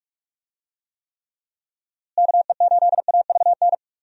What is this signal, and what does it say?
Morse code, heard as a keyed tone, about 44 wpm, 700 Hz. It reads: KE8AVN
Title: USA Call Signs - 6 Characters - Repeated 3x - 45wpm